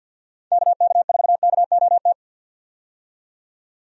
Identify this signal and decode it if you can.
Morse code, heard as a keyed tone, about 52 wpm, 700 Hz. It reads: KK4KOT
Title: USA Call Signs - 6 Characters - 50wpm